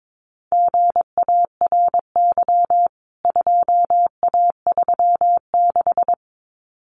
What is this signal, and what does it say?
Morse code, heard as a keyed tone, about 22 wpm, 700 Hz. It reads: GARY 2A36
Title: CWT Exchange 22wpm